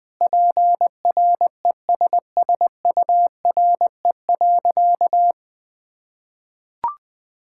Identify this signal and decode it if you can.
Morse code, heard as a keyed tone, about 20 wpm, 700 Hz. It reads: PRESSURE.
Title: Sentences from Top 1000 Words - Review Part 6 - 20wpm